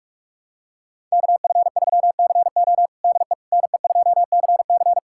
Morse code, heard as a keyed tone, about 46 wpm, 700 Hz. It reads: KF3CO DE NE2CC
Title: USA Callsign DE Callsign 45wpm